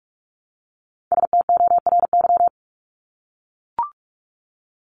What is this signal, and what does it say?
Morse code, heard as a keyed tone, about 44 wpm, 700 Hz. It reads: STORY
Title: Top 300 Words 45wpm